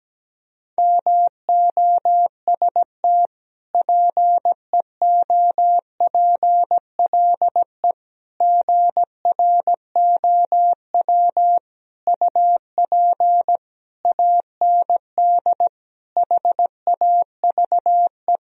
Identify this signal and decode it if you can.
Morse code, heard as a keyed tone, about 17 wpm, 700 Hz. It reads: MOST PEOPLE GROW UP AND HAVE
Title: Sentences from Top 1000 Words - Review Part 1 - 17wpm